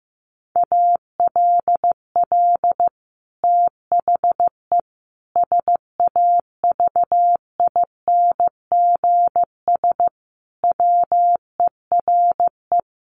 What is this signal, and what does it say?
Morse code, heard as a keyed tone, about 15 wpm, 700 Hz. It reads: ALL THE SAVINGS WERE
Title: Sentences from Top 1200 Words 15wpm